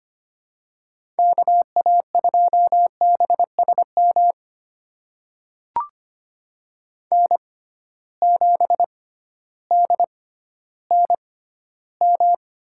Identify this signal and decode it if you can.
Morse code, heard as a keyed tone, about 25 wpm, spaced as for 13 wpm, 700 Hz. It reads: KA2BSM N7DNM
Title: USA Call Signs - 6x Character Spacing - 25wpm